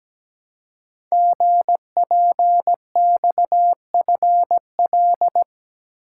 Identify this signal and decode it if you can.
Morse code, heard as a keyed tone, about 17 wpm, 700 Hz. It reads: GPXFL
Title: Code Groups - 5 Letters - 17wpm